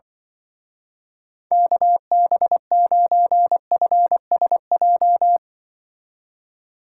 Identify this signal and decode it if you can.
Morse code, heard as a keyed tone, about 25 wpm, 700 Hz. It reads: KB9FSJ
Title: Speed Racing - USA 2x3 Call Signs 20wpm